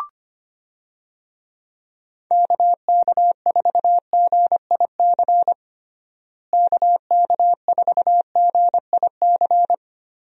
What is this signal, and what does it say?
Morse code, heard as a keyed tone, about 25 wpm, 700 Hz. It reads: KK4GIC KK4GIC
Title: USA Call Signs - 6 Characters - Repeated 2x - 25wpm